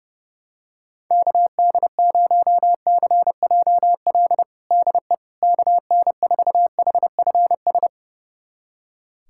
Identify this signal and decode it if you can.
Morse code, heard as a keyed tone, about 32 wpm, 700 Hz. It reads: KD0CJL DE KN4HFS
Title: USA Callsign DE Callsign 30wpm